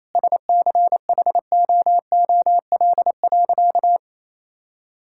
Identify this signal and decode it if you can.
Morse code, heard as a keyed tone, about 28 wpm, 700 Hz. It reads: SCHOOL.
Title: Sentences from Top 1000 Words - Review Part 6 - 28wpm